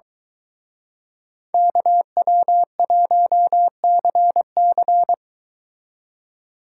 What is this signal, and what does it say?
Morse code, heard as a keyed tone, about 23 wpm, 700 Hz. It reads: KW1CC